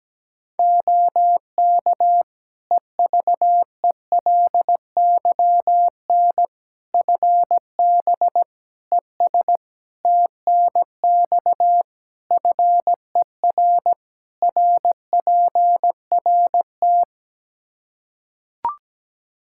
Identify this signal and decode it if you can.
Morse code, heard as a keyed tone, about 17 wpm, 700 Hz. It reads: OK EVELYN FB ES TNX FER RPRT
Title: Practice QSO - Exchange 2 - Info - 17wpm